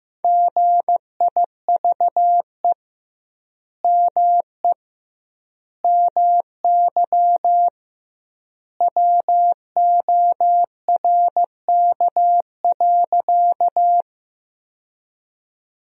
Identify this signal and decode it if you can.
Morse code, heard as a keyed tone, about 15 wpm, 700 Hz. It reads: GIVE ME MY WORK.
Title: 2x Word-Spacing - Sentences from Top 200 Words - 15wpm